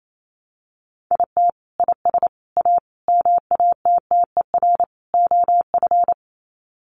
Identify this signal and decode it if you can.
Morse code, heard as a keyed tone, about 28 wpm, 700 Hz. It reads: IT IS A MATTER OF
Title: Sets of 5 Words Encore2 28wpm